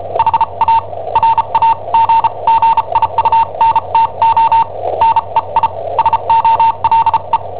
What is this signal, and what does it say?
Morse code, heard as a keyed tone, about 34 wpm, spaced as for 28 wpm, 910 Hz, with noise. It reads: HA RAGGIUNTO NEI SOLE
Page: Cassetta esercizi preparata da I7OHP per IW5ELP